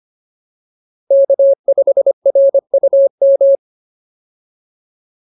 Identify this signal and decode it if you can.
Morse code, heard as a keyed tone, about 25 wpm, 550 Hz. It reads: K5RUM